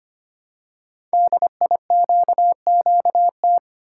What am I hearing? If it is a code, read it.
Morse code, heard as a keyed tone, about 25 wpm, 700 Hz. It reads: DIQQT
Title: Code Groups - 5 Letters - 25wpm